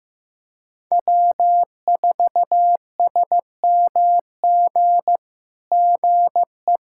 Morse code, heard as a keyed tone, about 16 wpm, 700 Hz. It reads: W4SMG GE
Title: SOTA QSO Practice 15wpm